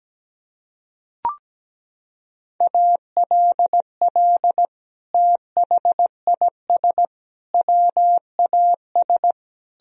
Morse code, heard as a keyed tone, about 17 wpm, 700 Hz. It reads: ALL THIS WAS
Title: Sentences from Top 1000 Words - Review Part 4 - 17wpm